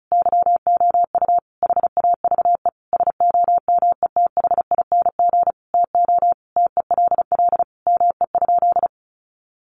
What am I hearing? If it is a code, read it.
Morse code, heard as a keyed tone, about 34 wpm, 700 Hz. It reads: YOU HAVE SOMETHING TO TELL ME?
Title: Sentences from Top 600 Words 35wpm